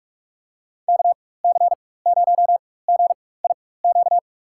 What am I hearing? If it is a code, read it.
Morse code, heard as a keyed tone, about 41 wpm, spaced as for 20 wpm, 700 Hz. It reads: KC0GIQ
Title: USA Call Signs - 6 Characters - 4x Character Spacing - 45wpm